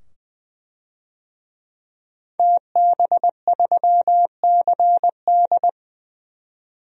Morse code, heard as a keyed tone, about 21 wpm, 700 Hz. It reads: TB3CD
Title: DXCC-100 Call Signs - 5 Characters - 20wpm